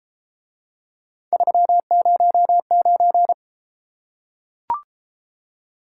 Morse code, heard as a keyed tone, about 31 wpm, 700 Hz. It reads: 309